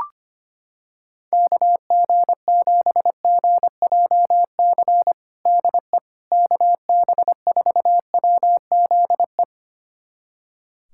Morse code, heard as a keyed tone, about 25 wpm, 700 Hz. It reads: KG7GJC DE KB4WZE
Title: USA Callsign DE Callsign 25wpm